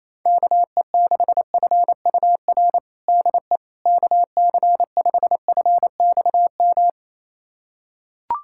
Morse code, heard as a keyed tone, about 28 wpm, 700 Hz. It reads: KE6FUR DE KC5FXM
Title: USA Callsign DE Callsign 28wpm